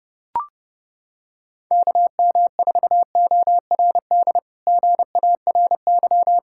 Morse code, heard as a keyed tone, about 30 wpm, 700 Hz. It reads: KM4ORD GARY
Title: K1USN SST QSO Practice - 30wpm